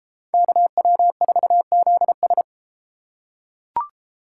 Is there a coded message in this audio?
Yes: Morse code from 0.3 to 2.4 s, heard as a keyed tone, about 31 wpm, 700 Hz. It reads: KW4ZS